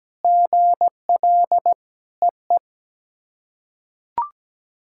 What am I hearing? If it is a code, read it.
Morse code, heard as a keyed tone, about 17 wpm, 700 Hz. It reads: GL EE
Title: SOTA QSO Practice 17wpm